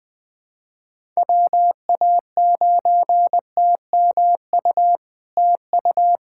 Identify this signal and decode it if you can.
Morse code, heard as a keyed tone, about 21 wpm, 700 Hz. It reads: WA9TMU TU
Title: POTA QSO Practice - 20wpm